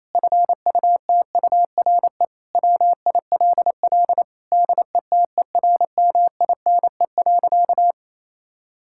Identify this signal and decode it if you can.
Morse code, heard as a keyed tone, about 28 wpm, 700 Hz. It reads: FUTURE WILL DETERMINE.
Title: Sentences from Top 1000 Words - Review Part 5 - 28wpm